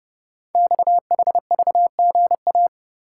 Morse code, heard as a keyed tone, about 30 wpm, 700 Hz. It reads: XHVGA
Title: Code Groups - 5 Letters - 30wpm